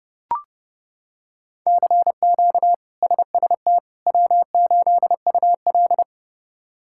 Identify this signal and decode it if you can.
Morse code, heard as a keyed tone, about 32 wpm, 700 Hz. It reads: CQ SST W8UL
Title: K1USN SST QSO Practice - 30wpm